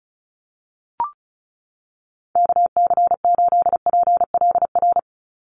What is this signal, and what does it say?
Morse code, heard as a keyed tone, about 34 wpm, 700 Hz. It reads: KC8PLR
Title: USA Call Signs 35wpm